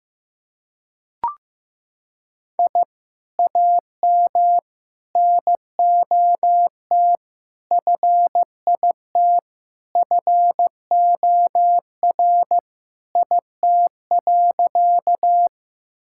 Morse code, heard as a keyed tone, about 15 wpm, 700 Hz. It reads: I AM NOT FIT FOR IT.
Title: Sentences from Top 1000 Words - Review Part 5 - 15wpm